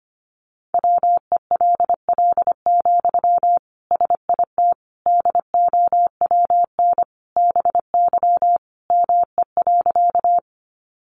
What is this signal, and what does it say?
Morse code, heard as a keyed tone, about 25 wpm, 700 Hz. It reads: WELL, SIT DOWN BY ME.
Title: Sentences from Top 500 Words 25wpm